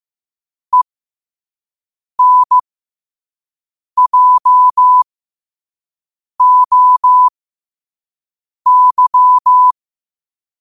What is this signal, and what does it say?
Morse code, heard as a keyed tone, about 15 wpm, spaced as for 5 wpm, 1000 Hz. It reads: ENJOY
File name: qotdfiles / 5wpm